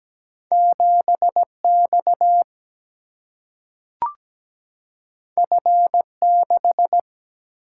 Morse code, heard as a keyed tone, about 17 wpm, 700 Hz. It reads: 7X F6
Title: Single-Letter Single-Number 17wpm